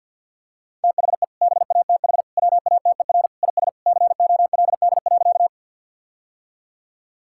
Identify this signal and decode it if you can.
Morse code, heard as a keyed tone, about 50 wpm, 700 Hz. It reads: THE BATH WATER IS COLD.
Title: Sentences from Top 1600 Words 50wpm